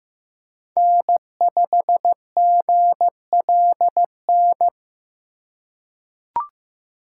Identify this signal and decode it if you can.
Morse code, heard as a keyed tone, about 16 wpm, 700 Hz. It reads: N5GLN
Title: USA Call Signs - 5 Characters - 4x Character Spacing - 15wpm